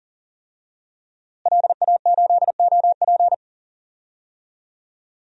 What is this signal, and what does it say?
Morse code, heard as a keyed tone, about 40 wpm, 700 Hz. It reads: LA8OP
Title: DXCC-100 Call Signs - 5 Characters - Repeated 3x - 40wpm